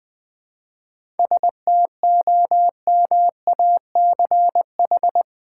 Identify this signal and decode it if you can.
Morse code, heard as a keyed tone, about 20 wpm, 700 Hz. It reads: STOMACH
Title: 7 Letter Words - First 3 Letters Given - 20wpm